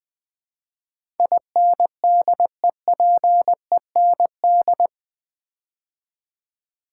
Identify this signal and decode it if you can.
Morse code, heard as a keyed tone, about 20 wpm, 700 Hz. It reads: INDEPEND
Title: Build A Word - 4 Letters At A Time - 11-Letter Words - 20wpm